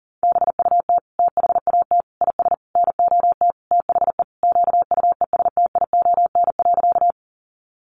Morse code, heard as a keyed tone, about 40 wpm, 700 Hz. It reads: BUT THAT IS NOT THE QUESTION.
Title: Sentences from Top 600 Words 40wpm